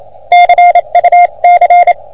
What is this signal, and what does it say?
Morse code, heard as a keyed tone, about 29 wpm, spaced as for 25 wpm, 710 Hz. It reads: CUC
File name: OM3PA PILUP CUC 388